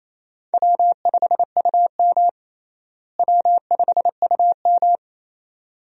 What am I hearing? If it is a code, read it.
Morse code, heard as a keyed tone, about 28 wpm, 700 Hz. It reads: W5UM W5UM
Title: USA Call Signs - 4 Characters - Repeated 2x - 28wpm